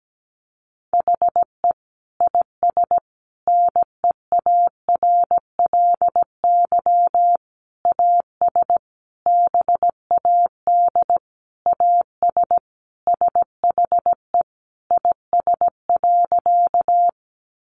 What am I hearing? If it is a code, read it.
Morse code, heard as a keyed tone, about 17 wpm, 700 Hz. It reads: HE IS NEARLY AS BAD AS SHE IS.
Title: Sentences from Top 1300 Words 17wpm